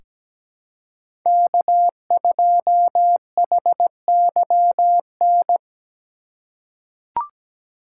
Morse code, heard as a keyed tone, about 17 wpm, 700 Hz. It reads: K2HYN
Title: USA Call Signs - 2x Character Spacing - 17wpm